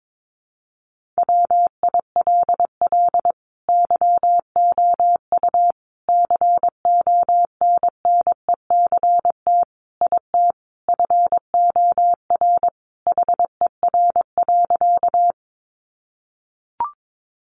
Morse code, heard as a keyed tone, about 22 wpm, 700 Hz. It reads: WILL YOU CONNECT IT FOR HER.
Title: Sentences from Top 1500 Words 22wpm